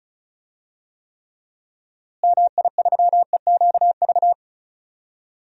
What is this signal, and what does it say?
Morse code, heard as a keyed tone, about 34 wpm, 700 Hz. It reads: MI3EQV
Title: DXCC-100 Call Signs - 6 Characters - Repeated 3x - 35wpm